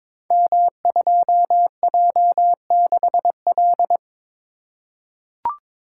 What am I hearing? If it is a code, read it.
Morse code, heard as a keyed tone, about 22 wpm, 700 Hz. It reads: M2J6L
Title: Code Groups - 5 Letter:Numbers - 22wpm